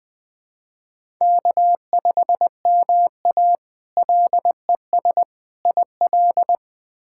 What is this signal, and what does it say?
Morse code, heard as a keyed tone, about 20 wpm, 700 Hz. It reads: K5MA LES IL
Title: K1USN SST QSO Practice - 20wpm